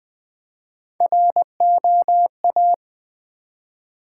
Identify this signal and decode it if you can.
Morse code, heard as a keyed tone, about 20 wpm, 700 Hz. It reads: ROA
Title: Build A Word - 1 Letter At A Time - 3 to 5-Letter Words - 20wpm